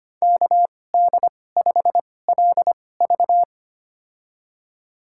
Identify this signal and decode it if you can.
Morse code, heard as a keyed tone, about 25 wpm, spaced as for 18 wpm, 700 Hz. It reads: KD5LV
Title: USA Call Signs - 5 Characters - 2x Character Spacing - 25wpm